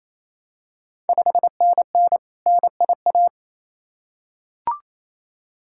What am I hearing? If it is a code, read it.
Morse code, heard as a keyed tone, about 28 wpm, 700 Hz. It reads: 5NN NIA
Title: Ontario QSO Party Exchange 28wpm